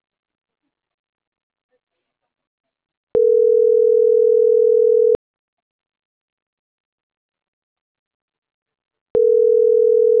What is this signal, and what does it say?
Ringback tone (North America), 440 + 480 Hz.